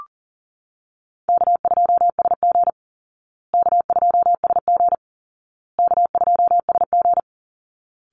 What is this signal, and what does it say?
Morse code, heard as a keyed tone, about 40 wpm, 700 Hz. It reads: K2SG K2SG K2SG
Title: Call Signs Repeated 2x - 40wpm